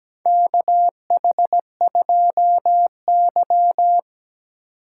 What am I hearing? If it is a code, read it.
Morse code, heard as a keyed tone, about 17 wpm, 700 Hz. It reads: KH2Y
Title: USA Call Signs - 4x Character Spacing - 17wpm